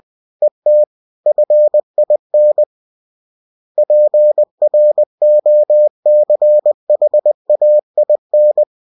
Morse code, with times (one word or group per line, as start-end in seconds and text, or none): 0.4-0.8 ET
1.3-2.6 FIN
3.8-8.6 PROCHAIN